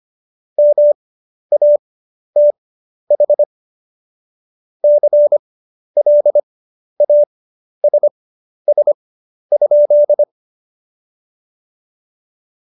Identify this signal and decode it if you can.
Morse code, heard as a keyed tone, about 25 wpm, spaced as for 11 wpm, 590 Hz. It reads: MATH CLASS?
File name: Jokes-25-char-10-eff